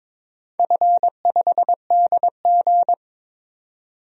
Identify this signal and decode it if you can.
Morse code, heard as a keyed tone, about 22 wpm, 700 Hz. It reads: F5DG
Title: DXCC-100 Call Signs - 4 Characters - 22wpm